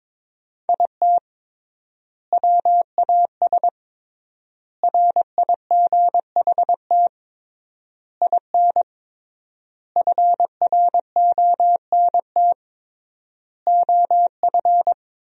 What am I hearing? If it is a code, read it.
Morse code, heard as a keyed tone, about 22 wpm, 700 Hz. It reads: IT WAS RIGHT IN FRONT OF
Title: 3x Word-Spacing - Sentences from Top 500 Words - 22wpm